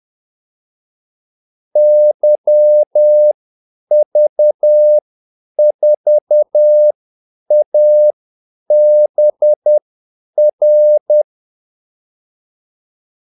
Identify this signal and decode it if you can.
Morse code, heard as a keyed tone, about 10 wpm, spaced as for 8 wpm, 600 Hz. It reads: YV4ABR
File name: Callsigns-10-8wpm